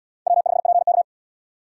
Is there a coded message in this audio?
Yes: Morse code from 0.3 to 1.0 s, heard as a keyed tone, about 74 wpm, 700 Hz. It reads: W5PK